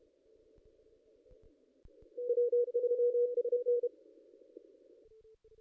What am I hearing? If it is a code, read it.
Morse code, heard as a keyed tone, about 30 wpm, 440 Hz. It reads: W3SN
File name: contests / ss-cw-k5tr-op-n5rz -2200